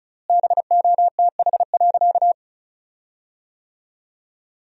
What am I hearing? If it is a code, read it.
Morse code, heard as a keyed tone, about 36 wpm, 700 Hz. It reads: BOTH.
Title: 4x Word-Spacing - Sentences from Top 400 Words - 35wpm